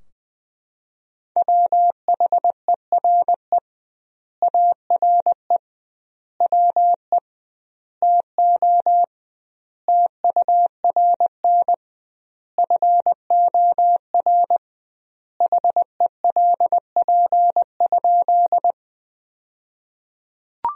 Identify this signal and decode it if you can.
Morse code, heard as a keyed tone, about 20 wpm, 700 Hz. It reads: WHERE ARE WE TO TURN FOR HELP?
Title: 2x Word-Spacing - Sentences from Top 200 Words - 20wpm